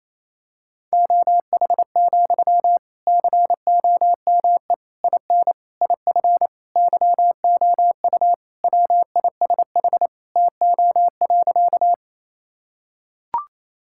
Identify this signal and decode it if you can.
Morse code, heard as a keyed tone, about 28 wpm, 700 Hz. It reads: OH, COME IN IF YOU WISH TO.
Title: Sentences from Top 1600 Words 28wpm